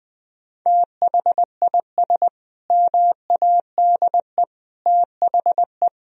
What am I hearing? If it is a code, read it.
Morse code, heard as a keyed tone, about 19 wpm, 700 Hz. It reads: THIS MADE THE
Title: Sentences from Top 1000 Words - Review Part 2 - 20wpm